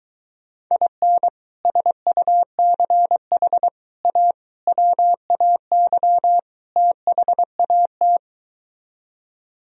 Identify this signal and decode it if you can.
Morse code, heard as a keyed tone, about 23 wpm, 700 Hz. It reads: IN SUCH A WAY THAT